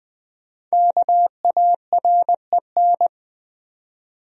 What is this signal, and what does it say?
Morse code, heard as a keyed tone, about 20 wpm, 700 Hz. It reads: KAREN
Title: USA Common Names - 20wpm